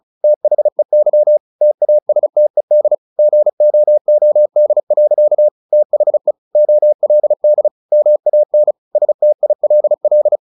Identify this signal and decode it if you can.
Morse code, heard as a keyed tone, about 34 wpm, 600 Hz. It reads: THEY TASTED GOOD. THE OLD MAN STILL
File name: the old man and the sea 2 35wpm